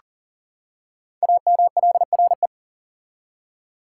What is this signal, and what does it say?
Morse code, heard as a keyed tone, about 39 wpm, 700 Hz. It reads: AMLRE
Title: Code Groups - 5 Letters - 40wpm